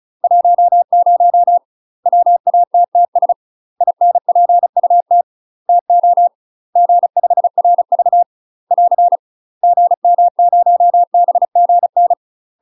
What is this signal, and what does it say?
Morse code, heard as a keyed tone, about 35 wpm, 700 Hz. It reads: 10 WATTS INPUT TO G5RV <AR> GM0BGN
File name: UK MASTER QSOs TRAINING / 35WPM / UK MASTER QSOs 0034